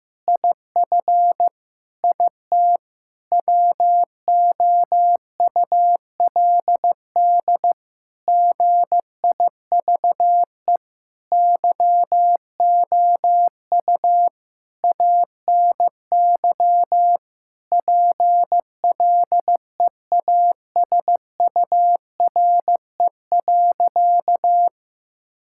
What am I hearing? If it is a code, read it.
Morse code, heard as a keyed tone, about 15 wpm, 700 Hz. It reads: IF IT WOULD GIVE YOU ANY PLEASURE.
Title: Sentences from Top 1600 Words 15wpm